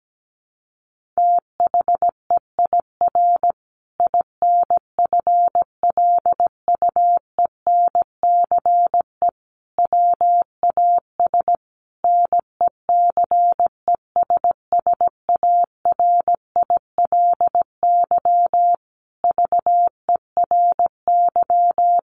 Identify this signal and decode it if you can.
Morse code, heard as a keyed tone, about 17 wpm, 700 Hz. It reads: THEIR INFLUENCE WAS NECESSARILY VERY
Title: Sentences from Top 1400 Words 17wpm